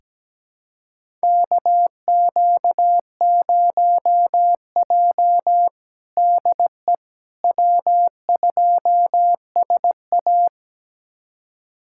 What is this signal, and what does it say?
Morse code, heard as a keyed tone, about 17 wpm, 700 Hz. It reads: KQ0J DE W2SA
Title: USA Callsign DE Callsign - 4 Characters - 17wpm